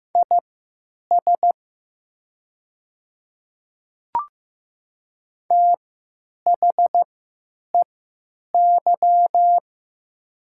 Morse code, heard as a keyed tone, about 15 wpm, spaced as for 9 wpm, 700 Hz. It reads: IS THEY
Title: 3x Character Spacing - Top 500 Words - 15wpm